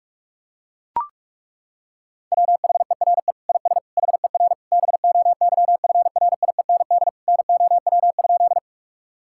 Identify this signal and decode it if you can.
Morse code, heard as a keyed tone, about 41 wpm, 700 Hz. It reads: WHERE IS HER BOYFRIEND NOW?
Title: Sentences from Top 1000 Words - Review Part 5 - 45wpm